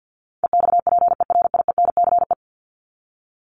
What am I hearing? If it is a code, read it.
Morse code, heard as a keyed tone, about 50 wpm, 700 Hz. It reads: EXPERIENCE
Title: Top 501-750 Words 50wpm